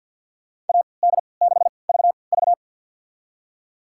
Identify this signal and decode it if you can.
Morse code, heard as a keyed tone, about 52 wpm, spaced as for 29 wpm, 700 Hz. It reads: AD6VV
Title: USA Call Signs - 5 Characters - 3x Character Spacing - 50wpm